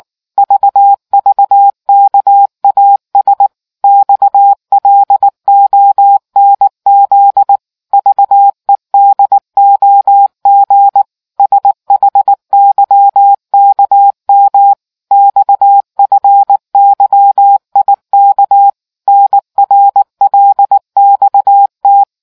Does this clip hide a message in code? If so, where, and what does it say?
Yes: Morse code from 0.4 to 22.0 s, heard as a keyed tone, about 19 wpm, 800 Hz. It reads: VVKAS XLONZ VEDOG SHYKM XFYIK NRLXT